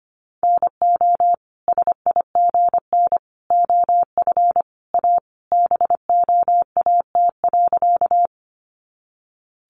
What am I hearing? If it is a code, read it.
Morse code, heard as a keyed tone, about 25 wpm, 700 Hz. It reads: NO SIGN OF A BOAT.
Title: Sentences from Top 1000 Words 25wpm